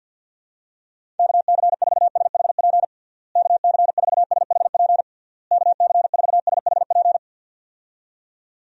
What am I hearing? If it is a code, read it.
Morse code, heard as a keyed tone, about 52 wpm, 700 Hz. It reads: KC4SHP KC4SHP KC4SHP
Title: USA Call Signs - 6 Characters - Repeated 3x - 50wpm